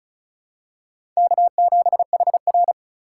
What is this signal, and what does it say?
Morse code, heard as a keyed tone, about 35 wpm, 700 Hz. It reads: K7HR
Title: USA Call Signs - 4 Characters - 4x Character Spacing - 35wpm